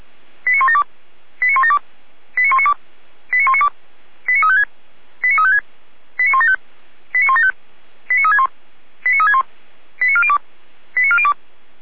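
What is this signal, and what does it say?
Five-tone selective call (ZVEI), heard as quick runs of five tones, tones of about 70 ms. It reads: 89172 89172 89192 89192 89356 89356 89175 89175 89361 89361 89402 89402
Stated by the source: ZVEI five-tone call received by SDR, from feuerwehr-orlamuende.org fire brigade audio archive.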